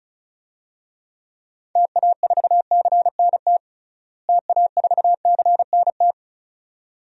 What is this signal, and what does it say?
Morse code, heard as a keyed tone, about 36 wpm, 700 Hz. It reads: TA4CNT TA4CNT
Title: DXCC-100 Call Signs - 6 Characters - Repeated 2x - 35wpm